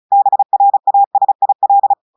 Morse code, heard as a keyed tone, about 36 wpm, 800 Hz. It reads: BRASIL
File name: audios / portugais / portugais texte en clair 1 vitesse 35 mots minutes